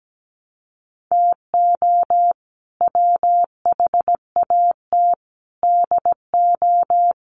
Morse code, heard as a keyed tone, about 17 wpm, 700 Hz. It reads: TO WHAT DO
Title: Sentences from Top 1000 Words 17wpm